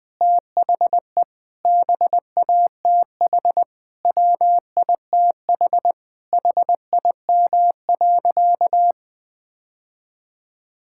Morse code, heard as a keyed tone, about 20 wpm, 700 Hz. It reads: THE BATH WITH HIM.
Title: Sentences from Top 1600 Words 20wpm